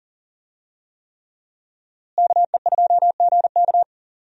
Morse code, heard as a keyed tone, about 37 wpm, 700 Hz. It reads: KE2GK
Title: USA Call Signs - 5 Characters - Repeated 3x - 40wpm